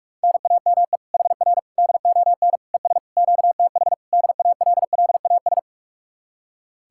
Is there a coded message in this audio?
Yes: Morse code from 0.2 to 5.6 s, heard as a keyed tone, about 58 wpm, 700 Hz. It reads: NAME HR DON ES QTH DALLAS